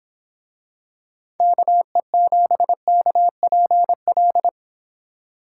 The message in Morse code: KE7KPL

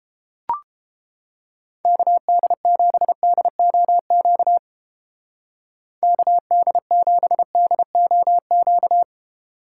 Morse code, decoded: KD7DOQ KD7DOQ

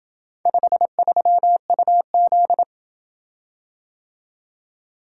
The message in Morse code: 53UZ

